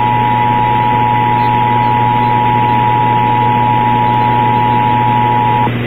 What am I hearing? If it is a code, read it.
Emergency Alert System (SAME) tones (the 853 and 960 Hz attention tone), with noise. It is heard from 0.7 to 5.4 s.